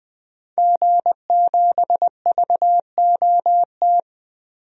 Morse code, heard as a keyed tone, about 20 wpm, 700 Hz. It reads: G7VOT
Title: DXCC-100 Call Signs - 5 Characters - 20wpm